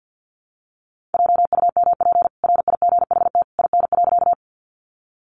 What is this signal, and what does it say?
Morse code, heard as a keyed tone, about 50 wpm, 700 Hz. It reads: JUMP RIGHT IN.
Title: Sentences from Top 1300 Words 50wpm